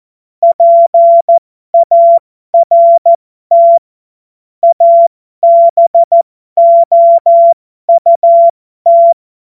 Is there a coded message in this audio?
Yes: Morse code from 0.4 to 9.1 s, heard as a keyed tone, about 14 wpm, spaced as for 12 wpm, 680 Hz. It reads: PART ABOUT